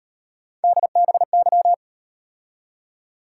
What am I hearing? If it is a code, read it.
Morse code, heard as a keyed tone, about 36 wpm, 700 Hz. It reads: DBY